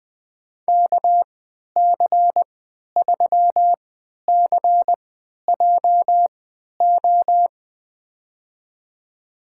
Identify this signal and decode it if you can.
Morse code, heard as a keyed tone, about 20 wpm, spaced as for 11 wpm, 700 Hz. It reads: KC3CJO
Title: USA Call Signs - 6 Characters - 3x Character Spacing - 20wpm